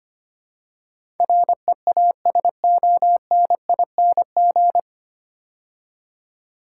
Morse code, heard as a keyed tone, about 25 wpm, 700 Hz. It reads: REASONING
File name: practice / 25wpm / 9 Letter Words - First 3 Letters Given - 25wpm